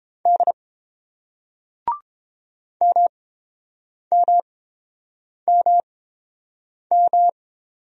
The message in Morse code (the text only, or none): D MMMM